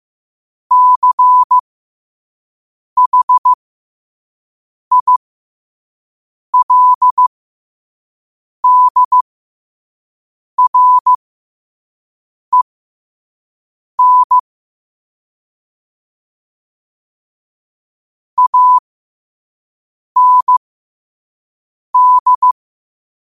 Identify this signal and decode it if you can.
Morse code, heard as a keyed tone, about 15 wpm, spaced as for 5 wpm, 1000 Hz. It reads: CHILDREN AND
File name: 5wpm